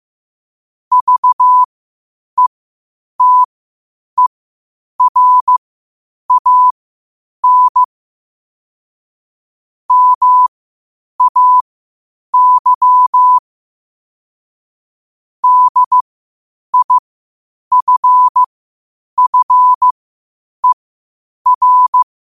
Morse code, heard as a keyed tone, about 15 wpm, spaced as for 8 wpm, 1000 Hz. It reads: VETERAN MAY DIFFER
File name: qotdfiles / 8wpm